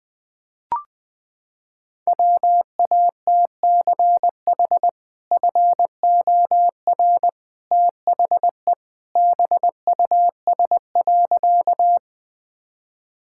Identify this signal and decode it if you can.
Morse code, heard as a keyed tone, about 21 wpm, 700 Hz. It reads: WATCH FOR THE BUS.
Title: Sentences from Top 1000 Words - Review Part 6 - 20wpm